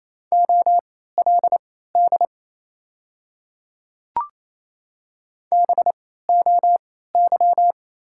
Morse code, heard as a keyed tone, about 28 wpm, spaced as for 16 wpm, 700 Hz. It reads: OLD BOY
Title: 3x Character Spacing - Top 200 Words - 28wpm